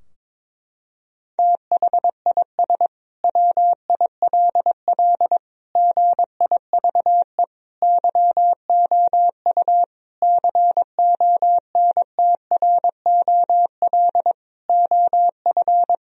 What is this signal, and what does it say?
Morse code, heard as a keyed tone, about 22 wpm, 700 Hz. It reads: THIS WILL GIVE YOU CONTROL OF
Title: Sentences from Top 1000 Words - Review Part 3 - 22wpm